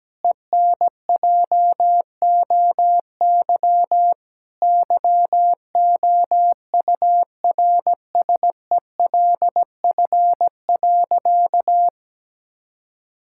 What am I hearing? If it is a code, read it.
Morse code, heard as a keyed tone, about 17 wpm, 700 Hz. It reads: ENJOY YOURSELF.
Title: Sentences from Top 1000 Words - Review Part 5 - 17wpm